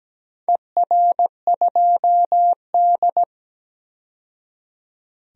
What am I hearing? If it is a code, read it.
Morse code, heard as a keyed tone, about 17 wpm, 700 Hz. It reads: ER2D
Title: DXCC-100 Call Signs - 4 Characters - 17wpm